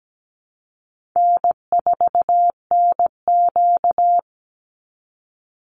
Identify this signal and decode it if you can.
Morse code, heard as a keyed tone, about 17 wpm, 700 Hz. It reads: N4NQ
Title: Call Signs Repeated 2x - 17wpm